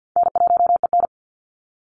Morse code, heard as a keyed tone, about 50 wpm, 700 Hz. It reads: N1EN
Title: Call Signs - 50wpm